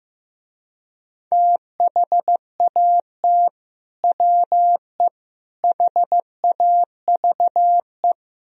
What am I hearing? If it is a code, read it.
Morse code, heard as a keyed tone, about 15 wpm, 700 Hz. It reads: THAT WE HAVE
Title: Speed-Racing - Sets of 3 Words 15wpm